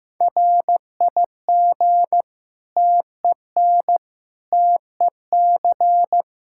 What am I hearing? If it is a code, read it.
Morse code, heard as a keyed tone, about 15 wpm, 700 Hz. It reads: RIG TEN TEC
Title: Practice QSO - Exchange 2 - Info - 15wpm